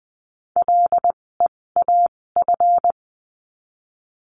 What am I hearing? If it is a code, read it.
Morse code, heard as a keyed tone, about 20 wpm, spaced as for 16 wpm, 700 Hz. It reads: LEAF
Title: 256 Lesson 42a - Words 12-20wpm